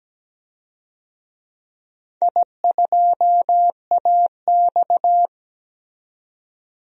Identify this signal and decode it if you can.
Morse code, heard as a keyed tone, about 17 wpm, 700 Hz. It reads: I2AX
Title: DXCC-100 Call Signs - 4 Characters - 17wpm